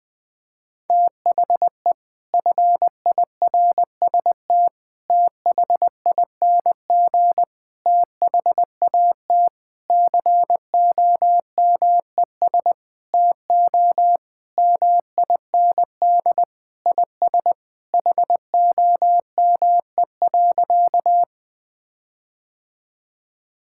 Morse code, heard as a keyed tone, about 20 wpm, 700 Hz. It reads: THE FIRST THING THAT COMES TO MIND IS HOME.
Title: Sentences with 7-Word Phrases - Spoken Beginning - 20wpm